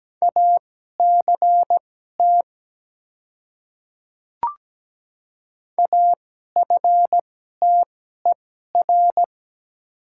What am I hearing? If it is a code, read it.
Morse code, heard as a keyed tone, about 17 wpm, spaced as for 12 wpm, 700 Hz. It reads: ACT AFTER
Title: 2x Character Spacing - Top 200 Words - 17wpm